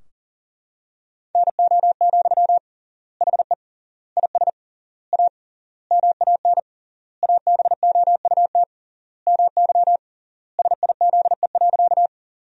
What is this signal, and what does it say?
Morse code, heard as a keyed tone, about 40 wpm, 700 Hz. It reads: NO, HE IS A MAN ABOUT MY SIZE.
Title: 3x Word-Spacing - Sentences from Top 400 Words - 40wpm